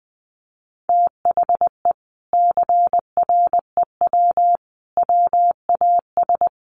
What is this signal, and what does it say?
Morse code, heard as a keyed tone, about 20 wpm, 700 Hz. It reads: THE CREW WAS
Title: Sentences from Top 1500 Words 20wpm